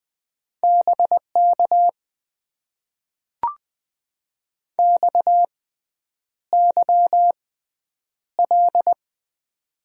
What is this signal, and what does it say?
Morse code, heard as a keyed tone, about 20 wpm, spaced as for 9 wpm, 700 Hz. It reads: BK XYL
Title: Top 90 QSO Elements - 6x Character Spacing - 20wpm